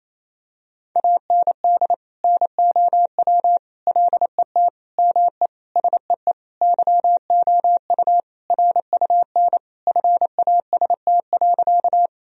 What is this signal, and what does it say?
Morse code, heard as a keyed tone, about 28 wpm, 700 Hz. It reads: AND NOW LET ME SEE YOU RUN FAST.
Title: Sentences from Top 1000 Words - Review Part 2 - 28wpm